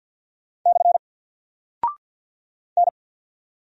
Morse code, heard as a keyed tone, about 50 wpm, 700 Hz. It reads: / N
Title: Single Character (Letter:Number:Punctuation:Prosign) 50wpm